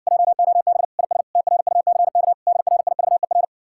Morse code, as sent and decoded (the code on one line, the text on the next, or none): .--- --- -... / .. ... / - .-. ..- -.-. -.- / -.. .-. .. ...- . .-.
JOB IS TRUCK DRIVER